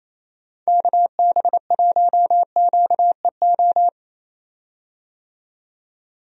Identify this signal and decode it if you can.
Morse code, heard as a keyed tone, about 28 wpm, 700 Hz. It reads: KB1QEO
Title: USA Call Signs - 6 Characters - 28wpm